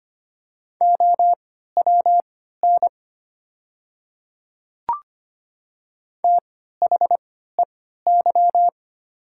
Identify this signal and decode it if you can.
Morse code, heard as a keyed tone, about 25 wpm, spaced as for 14 wpm, 690 Hz. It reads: OWN THEY
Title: 3x Character Spacing - Top 300 Words - 25wpm